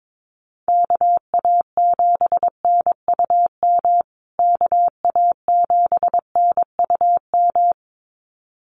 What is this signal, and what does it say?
Morse code, heard as a keyed tone, about 22 wpm, 700 Hz. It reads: KA7NUM KA7NUM
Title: USA Call Signs Repeated 2x 22wpm